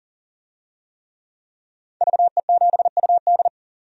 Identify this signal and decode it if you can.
Morse code, heard as a keyed tone, about 40 wpm, 700 Hz. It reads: VE7UD
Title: DXCC-100 Call Signs - 5 Characters - 40wpm